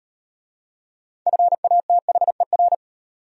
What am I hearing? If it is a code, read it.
Morse code, heard as a keyed tone, about 33 wpm, 700 Hz. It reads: FATHER